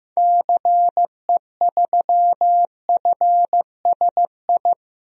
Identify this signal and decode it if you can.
Morse code, heard as a keyed tone, about 15 wpm, 700 Hz. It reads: CE3FSI
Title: DXCC-100 Call Signs - 6 Characters - Repeated 2x - 15wpm